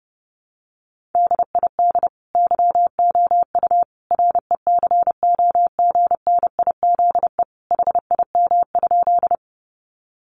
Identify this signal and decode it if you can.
Morse code, heard as a keyed tone, about 30 wpm, 700 Hz. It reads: DID YOU RECOGNIZE HIM?
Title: Sentences from Top 1400 Words 30wpm